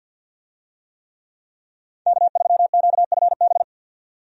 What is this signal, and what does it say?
Morse code, heard as a keyed tone, about 52 wpm, 700 Hz. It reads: K3QFB
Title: USA Call Signs - 5 Characters - 50wpm